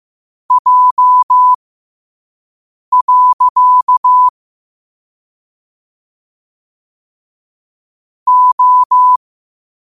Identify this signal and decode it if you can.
Morse code, heard as a keyed tone, about 15 wpm, spaced as for 5 wpm, 1000 Hz. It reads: J. O
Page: Quotes for Fri, 15 Aug 2025 in Morse Code at 5 words per minute.